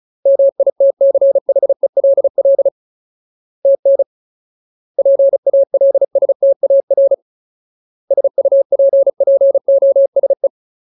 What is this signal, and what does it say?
Morse code, heard as a keyed tone, about 36 wpm, 550 Hz. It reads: MITCHELL TN PALSTAR SUPPOSE
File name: Ultimate Word List / 35WPM Ultimate